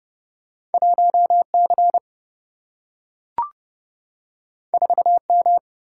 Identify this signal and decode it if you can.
Morse code, heard as a keyed tone, about 30 wpm, 700 Hz. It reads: 1C 4M
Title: Single-Letter Single-Number 30wpm